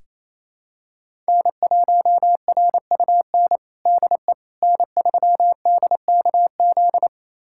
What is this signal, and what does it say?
Morse code, heard as a keyed tone, about 28 wpm, 700 Hz. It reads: N1RUN DE N3DKZ
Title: USA Callsign DE Callsign - 5 Characters - 28wpm